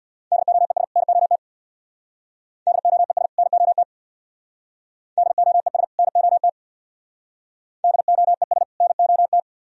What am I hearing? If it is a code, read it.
Morse code, heard as a keyed tone, about 61 wpm, 700 Hz. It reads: DOES NOT DOES NOT DOES NOT DOES NOT
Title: Speed-Racing - Sets of 2 Words 50wpm